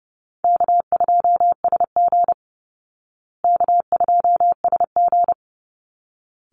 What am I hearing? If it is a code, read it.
Morse code, heard as a keyed tone, about 30 wpm, 700 Hz. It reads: K2SG K2SG
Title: Call Signs Repeated 3x - 30wpm